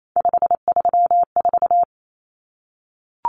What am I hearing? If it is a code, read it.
Morse code, heard as a keyed tone, about 28 wpm, 700 Hz. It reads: 534
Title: Common Cut Numbers - 3-Digit 28wpm